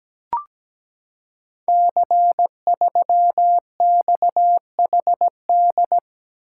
Morse code, heard as a keyed tone, about 17 wpm, 700 Hz. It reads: C3XHD
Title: Code Groups - 5 Letter:Numbers - 17wpm